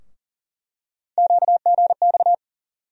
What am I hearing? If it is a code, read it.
Morse code, heard as a keyed tone, about 40 wpm, 700 Hz. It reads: QGX
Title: USA Call Sign Postfixes - 3 Letters 40wpm